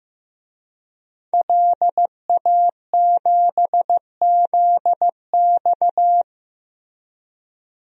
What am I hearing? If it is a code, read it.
Morse code, heard as a keyed tone, about 15 wpm, 700 Hz. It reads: LA7ZX
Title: DXCC-100 Call Signs - 5 Characters - Repeated 2x - 15wpm